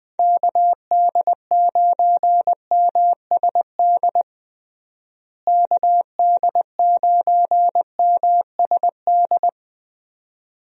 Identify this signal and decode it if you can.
Morse code, heard as a keyed tone, about 20 wpm, 700 Hz. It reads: KD9MSD KD9MSD
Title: USA Call Signs - 6 Characters - Repeated 2x - 20wpm